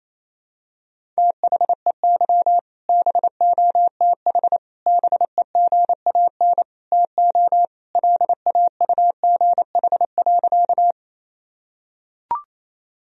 Morse code, heard as a keyed tone, about 28 wpm, 700 Hz. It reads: THEY BOTH BEGAN TO LAUGH.
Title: Sentences from Top 1000 Words - Review Part 1 - 28wpm